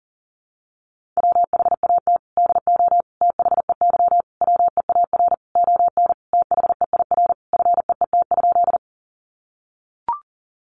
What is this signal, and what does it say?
Morse code, heard as a keyed tone, about 40 wpm, 700 Hz. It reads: WHAT DO THEY WEAR ON THEIR FEET?
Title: Sentences from Top 1500 Words 40wpm